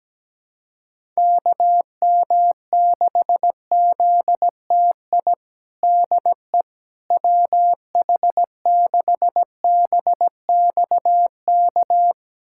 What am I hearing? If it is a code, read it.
Morse code, heard as a keyed tone, about 17 wpm, 700 Hz. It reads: KM6ZTI DE WH6BXK
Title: USA Callsign DE Callsign - 6 Characters - 17wpm